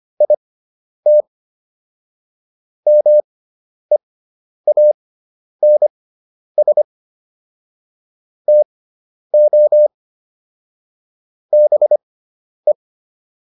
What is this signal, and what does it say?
Morse code, heard as a keyed tone, about 25 wpm, spaced as for 10 wpm, 600 Hz. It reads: IT MEANS TO BE